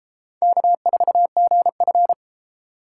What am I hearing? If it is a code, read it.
Morse code, heard as a keyed tone, about 30 wpm, 700 Hz. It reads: K4GF